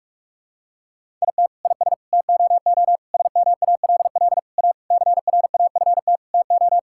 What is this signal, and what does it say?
Morse code, heard as a keyed tone, about 46 wpm, 700 Hz. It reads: IT IS TOO SMALL A CRAFT TO
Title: Sentences from Top 1000 Words - Review Part 6 - 45wpm